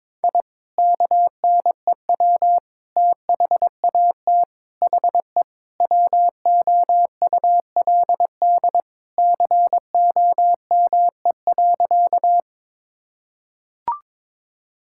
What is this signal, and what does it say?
Morse code, heard as a keyed tone, about 22 wpm, 700 Hz. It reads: I KNEW THAT HE WOULD COME.
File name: practice / 22wpm / Sentences with 2-Word Phrases - 22wpm